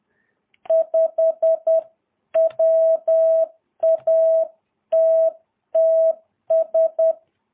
Morse code, heard as a keyed tone, about 10 wpm, 650 Hz. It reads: 5 WATTS
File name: jh8cbh / 2022cw